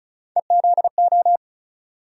Morse code, heard as a keyed tone, about 36 wpm, 700 Hz. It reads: EZO
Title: USA Call Sign Postfixes - 3 Letters 35wpm